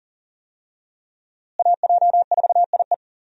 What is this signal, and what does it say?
Morse code, heard as a keyed tone, about 40 wpm, 700 Hz. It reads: AJ4IE
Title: USA Call Signs - 5 Characters - Repeated 3x - 40wpm